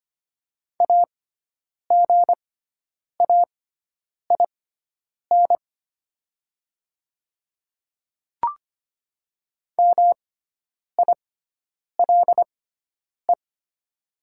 Morse code, heard as a keyed tone, about 25 wpm, spaced as for 9 wpm, 700 Hz. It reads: AGAIN MILE
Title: 6x Character Spacing - Top 500 Words - 25wpm